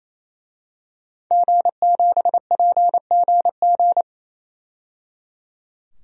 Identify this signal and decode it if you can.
Morse code, heard as a keyed tone, about 28 wpm, 700 Hz. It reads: G7PGG
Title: DXCC-100 Call Signs - 5 Characters - 28wpm